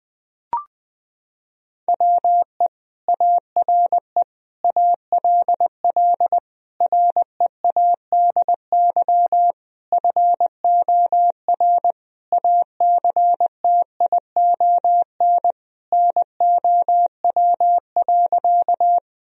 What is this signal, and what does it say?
Morse code, heard as a keyed tone, about 20 wpm, 700 Hz. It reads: WE ARE ALL READY FOR ACTION NOW.
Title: Sentences from Top 1000 Words - Review Part 4 - 20wpm